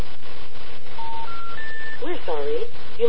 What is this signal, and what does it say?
Special information tone, with speech.